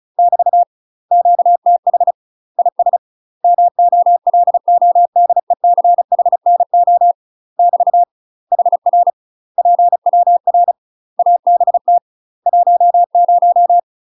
Morse code, heard as a keyed tone, about 36 wpm, 700 Hz. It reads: <BT> QTH IS MOLODECHNO <BT> HR PWR ABT 10
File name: UK MASTER QSOs TRAINING / 35WPM / UK MASTER QSOs 0028